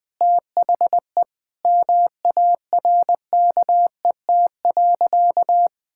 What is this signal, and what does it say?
Morse code, heard as a keyed tone, about 20 wpm, 700 Hz. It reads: THE MARKET.
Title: Sentences from Top 1000 Words - Review Part 6 - 20wpm